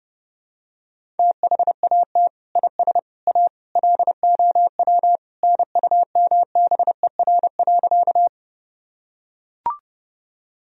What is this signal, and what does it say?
Morse code, heard as a keyed tone, about 30 wpm, 700 Hz. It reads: THAT IS A LOW NUMBER.
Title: Sentences from Top 1000 Words - Review Part 1 - 30wpm